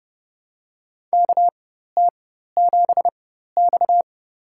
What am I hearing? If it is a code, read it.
Morse code, heard as a keyed tone, about 32 wpm, spaced as for 14 wpm, 700 Hz. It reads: KT7X
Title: USA Call Signs - 4 Characters - 4x Character Spacing - 30wpm